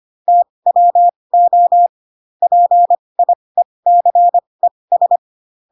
Morse code, heard as a keyed tone, about 25 wpm, spaced as for 20 wpm, 700 Hz. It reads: TWO PIECES
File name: Pinocchio / Pinocchio ch8 15 25 ik1hge